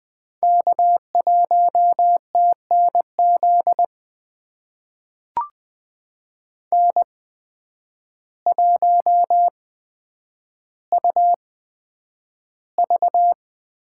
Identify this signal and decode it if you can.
Morse code, heard as a keyed tone, about 20 wpm, spaced as for 9 wpm, 700 Hz. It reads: K1TNZ N1UV
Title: USA Call Signs - 5 Characters - 8x Character Spacing - 20wpm